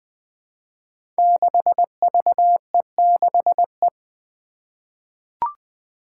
Morse code, heard as a keyed tone, about 20 wpm, 700 Hz. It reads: 6VE6E
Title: Five Challenging Letter-Number Combinations 20wpm